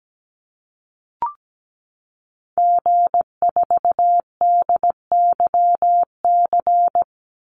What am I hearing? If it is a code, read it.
Morse code, heard as a keyed tone, about 17 wpm, 700 Hz. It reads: G4DYC
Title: Call Signs Repeated 3x - 17wpm